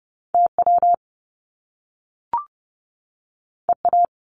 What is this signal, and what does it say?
Morse code, heard as a keyed tone, about 30 wpm, 700 Hz. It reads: TW EA
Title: Top Two Letter Combinations in Words 30wpm